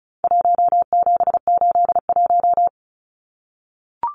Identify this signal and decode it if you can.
Morse code, heard as a keyed tone, about 34 wpm, 700 Hz. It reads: 1781
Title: Numbers - Contest Style - 4-Digit 35wpm